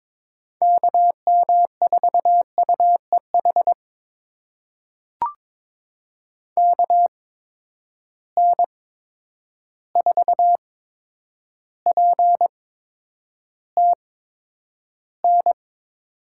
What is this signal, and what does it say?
Morse code, heard as a keyed tone, about 22 wpm, spaced as for 9 wpm, 700 Hz. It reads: KM4UEH KN4PTN
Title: USA Call Signs - 6 Characters - 8x Character Spacing - 22wpm